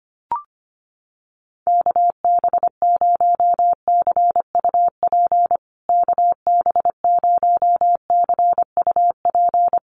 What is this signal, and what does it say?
Morse code, heard as a keyed tone, about 25 wpm, 700 Hz. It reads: KB0CUP KB0CUP
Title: USA Call Signs Repeated 3x 25wpm